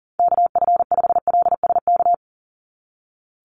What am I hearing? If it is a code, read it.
Morse code, heard as a keyed tone, about 40 wpm, 700 Hz. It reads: KF5LSK
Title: USA Call Signs 40wpm